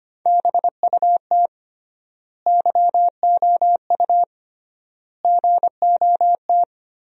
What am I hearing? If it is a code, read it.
Morse code, heard as a keyed tone, about 25 wpm, 700 Hz. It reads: BUT YOU GOT
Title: 3x Word-Spacing - Sentences from Top 300 Words - 25wpm